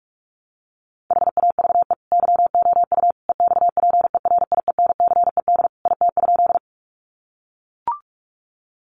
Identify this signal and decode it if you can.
Morse code, heard as a keyed tone, about 44 wpm, 700 Hz. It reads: HAVE YOU EXPERIENCED IT?
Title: Sentences from Top 1100 Words 45wpm